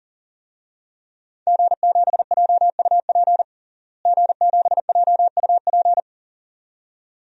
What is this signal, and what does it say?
Morse code, heard as a keyed tone, about 40 wpm, 700 Hz. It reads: G7JUP G7JUP
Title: DXCC-100 Call Signs - 5 Characters - Repeated 2x - 40wpm